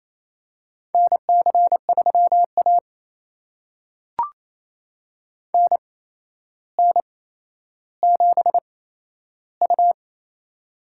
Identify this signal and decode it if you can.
Morse code, heard as a keyed tone, about 28 wpm, spaced as for 12 wpm, 700 Hz. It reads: NC3A NN7U
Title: USA Call Signs - 4 Characters - 8x Character Spacing - 28wpm